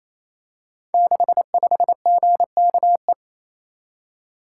Morse code, heard as a keyed tone, about 27 wpm, 700 Hz. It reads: 65GKE
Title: Code Groups - 5 Letter:Numbers - 28wpm